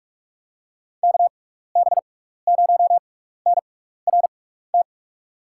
Morse code, heard as a keyed tone, about 44 wpm, spaced as for 15 wpm, 700 Hz. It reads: KB0NRT
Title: USA Call Signs - 6 Characters - 6x Character Spacing - 45wpm